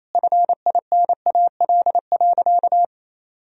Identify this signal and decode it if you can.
Morse code, heard as a keyed tone, about 28 wpm, 700 Hz. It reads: FINAL.
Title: Sentences from Top 1000 Words - Review Part 2 - 28wpm